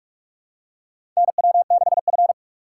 Morse code, heard as a keyed tone, about 47 wpm, 700 Hz. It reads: NW6F
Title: USA Call Signs - 4 Characters - 6x Character Spacing - 45wpm